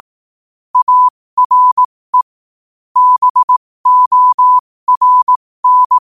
Morse code, heard as a keyed tone, about 18 wpm, spaced as for 15 wpm, 1000 Hz. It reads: ARE BORN
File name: qotdfiles / 15wpm